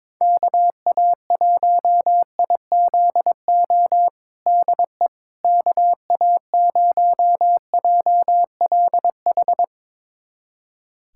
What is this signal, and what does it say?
Morse code, heard as a keyed tone, about 22 wpm, 700 Hz. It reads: KA1IZO DE KA0JLH
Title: USA Callsign DE Callsign - 6 Characters - 22wpm